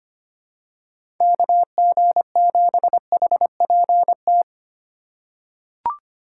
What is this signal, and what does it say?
Morse code, heard as a keyed tone, about 25 wpm, 700 Hz. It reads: KG7HPT
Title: USA Call Signs - 6 Characters - 2x Character Spacing - 25wpm